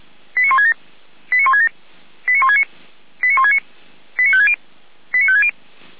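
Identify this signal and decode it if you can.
Five-tone selective call (ZVEI), heard as quick runs of five tones, tones of about 70 ms. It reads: 89167 89167 89169 89169 89560 89560
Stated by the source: ZVEI five-tone call 89220 received by SDR, logged by Feuerwehr Orlamünde.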